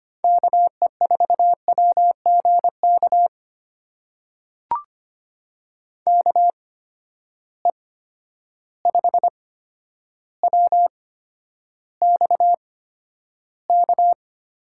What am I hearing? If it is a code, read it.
Morse code, heard as a keyed tone, about 25 wpm, spaced as for 11 wpm, 700 Hz. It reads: KE4WGK KE5WXK
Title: USA Call Signs - 6 Characters - 8x Character Spacing - 25wpm